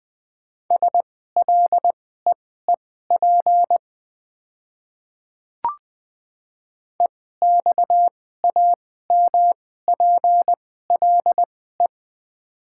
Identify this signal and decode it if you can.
Morse code, heard as a keyed tone, about 20 wpm, spaced as for 14 wpm, 700 Hz. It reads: SLEEP EXAMPLE
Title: 2x Character Spacing - Top 500 Words - 20wpm